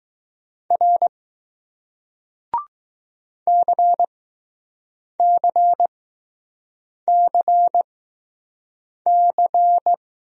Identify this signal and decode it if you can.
Morse code, heard as a keyed tone, about 20 wpm, spaced as for 6 wpm, 700 Hz. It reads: R CCCC